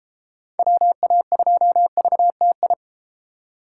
Morse code, heard as a keyed tone, about 33 wpm, 700 Hz. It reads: WA2VTI